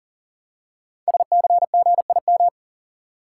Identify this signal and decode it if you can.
Morse code, heard as a keyed tone, about 42 wpm, 700 Hz. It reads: SCGIM
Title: Code Groups - 5 Letter:Numbers - 40wpm